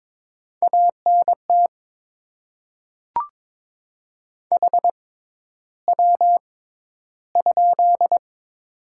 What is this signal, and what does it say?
Morse code, heard as a keyed tone, about 22 wpm, spaced as for 11 wpm, 690 Hz. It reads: ANT HW?
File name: Top 20 QSO Elements - 6x Character Spacing - 22wpm